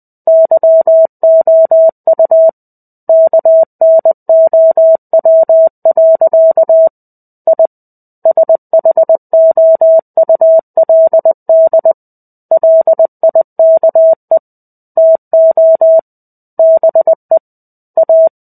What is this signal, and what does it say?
Morse code, heard as a keyed tone, about 20 wpm, 650 Hz. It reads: YOU KNOW. I SHOULD LIKE TO BE A
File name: 05-Chapter-05 20 WPM